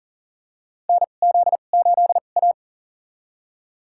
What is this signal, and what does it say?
Morse code, heard as a keyed tone, about 40 wpm, spaced as for 29 wpm, 700 Hz. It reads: NZ8A
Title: USA Call Signs - 4 Characters - 2x Character Spacing - 40wpm